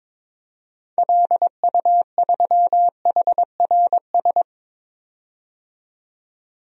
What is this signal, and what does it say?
Morse code, heard as a keyed tone, about 22 wpm, 700 Hz. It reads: LU3HRS
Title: DXCC-100 Call Signs - 6 Characters - 22wpm